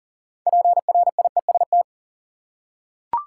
Morse code, heard as a keyed tone, about 40 wpm, 700 Hz. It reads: PRIEST
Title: Six Letter Words - 40wpm